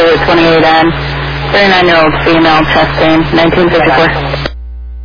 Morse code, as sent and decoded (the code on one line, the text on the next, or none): none